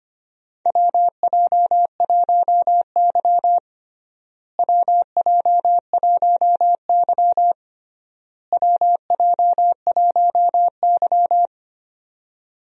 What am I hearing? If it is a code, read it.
Morse code, heard as a keyed tone, about 25 wpm, 700 Hz. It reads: WJ1Y WJ1Y WJ1Y
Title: USA Call Signs - 4 Characters - Repeated 3x - 25wpm